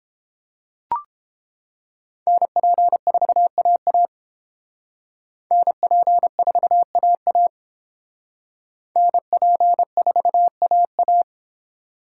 Morse code, heard as a keyed tone, about 29 wpm, 700 Hz. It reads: NP4AA NP4AA NP4AA